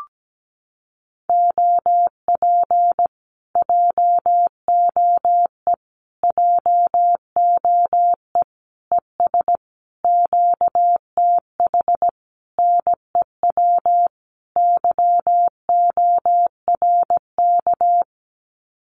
Morse code, heard as a keyed tone, about 17 wpm, 700 Hz. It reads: OP JOE JOE ES QTH NEW YORK
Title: Common QSO Phrases Encore 17wpm